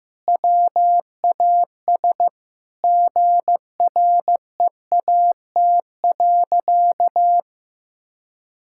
Morse code, heard as a keyed tone, about 15 wpm, 700 Hz. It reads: WAS GREAT.
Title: Sentences from Top 1000 Words - Review Part 6 - 15wpm